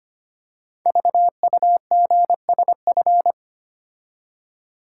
Morse code, heard as a keyed tone, about 25 wpm, 700 Hz. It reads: VUGSF
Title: Code Groups - 5 Letters - 25wpm